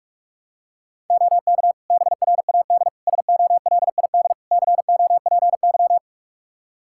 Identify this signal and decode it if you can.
Morse code, heard as a keyed tone, about 41 wpm, 700 Hz. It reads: OK BRAD SOLID COPY
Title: K1USN SST QSO Practice 45wpm